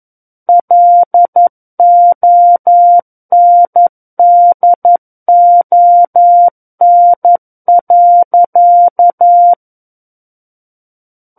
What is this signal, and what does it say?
Morse code, heard as a keyed tone, about 11 wpm, 700 Hz. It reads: LONDON.
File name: Book One / 02-Chapter 2 - 11 WPM